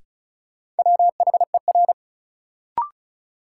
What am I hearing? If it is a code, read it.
Morse code, heard as a keyed tone, about 36 wpm, 700 Hz. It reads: WHER
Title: Sentences from Top 1500 Words - New Word Drill - 35wpm